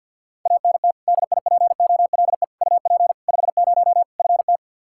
Morse code, heard as a keyed tone, about 51 wpm, 700 Hz. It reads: ANT DIPOLE UP 50 FT
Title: Practice QSO - Exchange 2 - Info - 50wpm